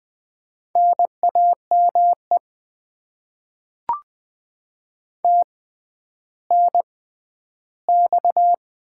Morse code, heard as a keyed tone, about 20 wpm, spaced as for 11 wpm, 700 Hz. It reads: NAME TNX
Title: Top 90 QSO Elements - 6x Character Spacing - 20wpm